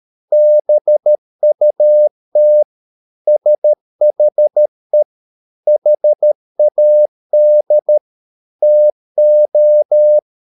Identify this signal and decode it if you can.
Morse code, heard as a keyed tone, about 13 wpm, 600 Hz. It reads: BUT SHE HAD TO